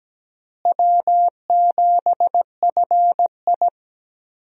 Morse code, heard as a keyed tone, about 17 wpm, 700 Hz. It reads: W7FI
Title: USA Call Signs - 4 Characters - Repeated 3x - 17wpm